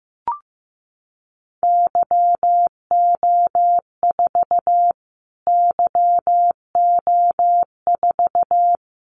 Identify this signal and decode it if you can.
Morse code, heard as a keyed tone, about 15 wpm, 700 Hz. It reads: YO4 YO4
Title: Callsign Prefixes for Top 25 DXCC - Repeated 3x - 15wpm